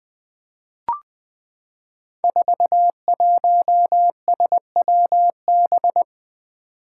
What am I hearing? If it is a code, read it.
Morse code, heard as a keyed tone, about 20 wpm, 700 Hz. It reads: 41SWB
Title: Five Challenging Letter-Number Combinations 20wpm